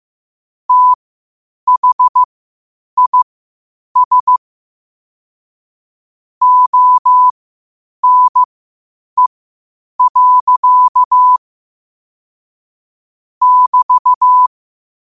Morse code, heard as a keyed tone, about 15 wpm, spaced as for 8 wpm, 1000 Hz. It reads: THIS ONE. <BT>
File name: qotdfiles / 8wpm